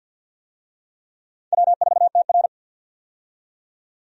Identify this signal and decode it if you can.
Morse code, heard as a keyed tone, about 53 wpm, 700 Hz. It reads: W4TR
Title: USA Call Signs - 4 Characters - 50wpm